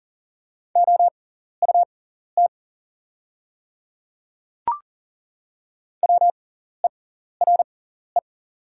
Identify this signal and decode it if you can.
Morse code, heard as a keyed tone, about 38 wpm, spaced as for 14 wpm, 690 Hz. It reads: OUT WERE
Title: 6x Character Spacing - Top 100 Words - 40wpm